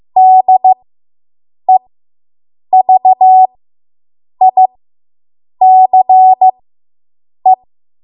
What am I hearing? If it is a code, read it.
Morse code, heard as a keyed tone, about 15 wpm, spaced as for 7 wpm, 750 Hz. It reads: DEVICE